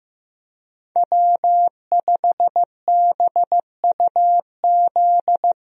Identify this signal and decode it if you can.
Morse code, heard as a keyed tone, about 16 wpm, 700 Hz. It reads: W5BUZ
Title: USA Call Signs - 5 Characters - Repeated 2x - 15wpm